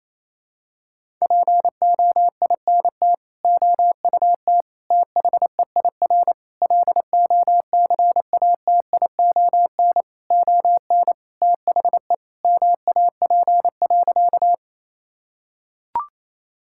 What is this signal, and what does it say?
Morse code, heard as a keyed tone, about 27 wpm, 700 Hz. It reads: POINT OUT THEIR LOCATION ON THE MAP.
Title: Sentences from Top 1000 Words - Review Part 6 - 28wpm